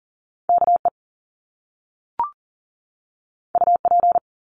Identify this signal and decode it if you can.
Morse code, heard as a keyed tone, about 40 wpm, 700 Hz. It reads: KE UP
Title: Top Two Letter Combinations in Words 40wpm